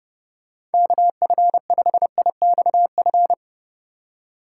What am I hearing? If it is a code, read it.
Morse code, heard as a keyed tone, about 30 wpm, 700 Hz. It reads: KF5IXF
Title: USA Call Signs - 6 Characters - 4x Character Spacing - 30wpm